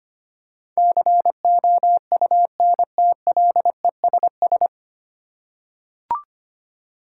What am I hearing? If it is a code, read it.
Morse code, heard as a keyed tone, about 25 wpm, 700 Hz. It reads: COUNTLESS
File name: practice / 25wpm / 9 Letter Words - First 3 Letters Given - 25wpm